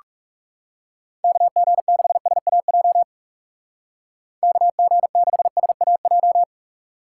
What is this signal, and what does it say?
Morse code, heard as a keyed tone, about 37 wpm, 700 Hz. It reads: KG6SAJ KG6SAJ
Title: Speed Racing - USA 2x3 Call Signs 30wpm